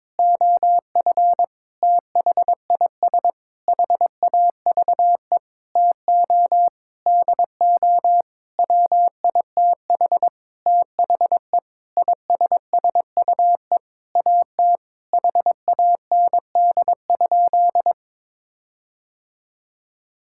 Morse code, heard as a keyed tone, about 22 wpm, 700 Hz. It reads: OF THIS HAVE TO DO WITH THE ISSUE AT HAND?
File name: Sentences with 9-Word Phrases - 22wpm